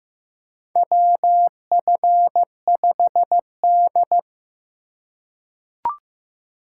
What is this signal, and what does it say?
Morse code, heard as a keyed tone, about 15 wpm, 700 Hz. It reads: WF5D
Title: USA Call Signs - 4 Characters - 3x Character Spacing - 15wpm